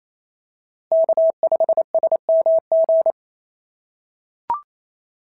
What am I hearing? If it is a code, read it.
Morse code, heard as a keyed tone, about 28 wpm, 680 Hz. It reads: K5SMG
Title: SOTA QSO Practice 28wpm